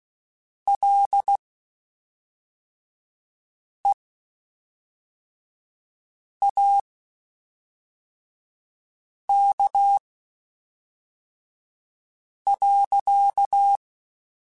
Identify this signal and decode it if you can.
Morse code, heard as a keyed tone, about 16 wpm, spaced as for 3 wpm, 800 Hz. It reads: LEAK.
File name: Quotes Spaced / Quotes Spaced-mp3s / Airtight sp @5